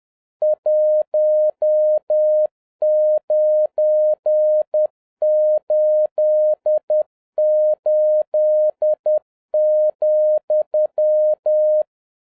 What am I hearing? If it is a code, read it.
Morse code, heard as a keyed tone, about 10 wpm, 610 Hz. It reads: 1988,